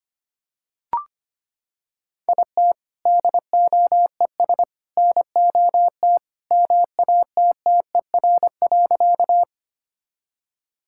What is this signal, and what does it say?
Morse code, heard as a keyed tone, about 25 wpm, 700 Hz. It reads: IT DOES NOT MATTER.
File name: Sentences with 3-Word Phrases - 25wpm